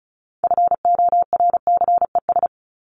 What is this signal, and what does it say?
Morse code, heard as a keyed tone, about 34 wpm, 700 Hz. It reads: FORCES
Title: ES:S Suffix - Top 100 Words with Base Words 35wpm